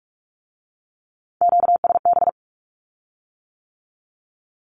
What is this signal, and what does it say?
Morse code, heard as a keyed tone, about 46 wpm, 700 Hz. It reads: QSB
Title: Q-Codes with Spoken Meaning - 45wpm